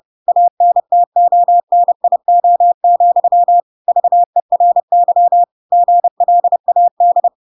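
Morse code, heard as a keyed tone, about 30 wpm, 700 Hz. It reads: ANTONIO, VERY GLAD
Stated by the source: Chapter 2 of The Adventures of Pinocchio read as a book in Morse code (CW) at 30 wpm, by ik1hge.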